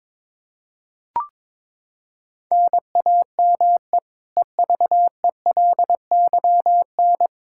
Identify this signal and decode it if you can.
Morse code, heard as a keyed tone, about 22 wpm, 700 Hz. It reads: NAME EVELYN
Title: Practice QSO - Exchange 1 - Beginning - 22wpm